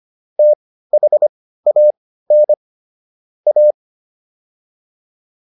Morse code, heard as a keyed tone, about 25 wpm, spaced as for 15 wpm, 600 Hz. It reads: THAN A